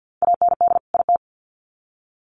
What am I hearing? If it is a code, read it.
Morse code, heard as a keyed tone, about 48 wpm, 700 Hz. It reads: AND IT
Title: Sets of 2 Words 50wpm